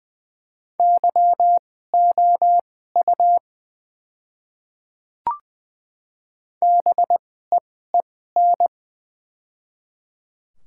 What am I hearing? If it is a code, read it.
Morse code, heard as a keyed tone, about 20 wpm, spaced as for 15 wpm, 700 Hz. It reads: YOU BEEN
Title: 2x Character Spacing - Top 100 Words - 20wpm